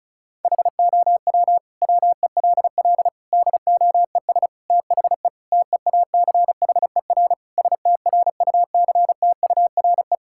Morse code, heard as a keyed tone, about 36 wpm, 700 Hz. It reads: HOW WELL DOES THE TEACHER STRUCTURE
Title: Sentences from Top 1000 Words - Review Part 6 - 35wpm